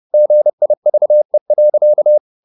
Morse code, heard as a keyed tone, about 30 wpm, 600 Hz. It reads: GIVE.
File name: the old man and the sea / the old man and the sea 1 30wpm